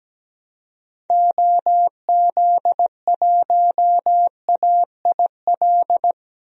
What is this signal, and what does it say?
Morse code, heard as a keyed tone, about 17 wpm, 700 Hz. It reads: OZ1AIL
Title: DXCC-100 Call Signs - 6 Characters - 17wpm